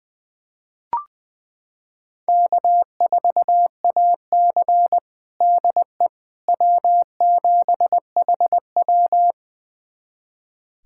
Morse code, heard as a keyed tone, about 20 wpm, 700 Hz. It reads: K4AC DE W7HW
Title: USA Callsign DE Callsign - 4 Characters - 20wpm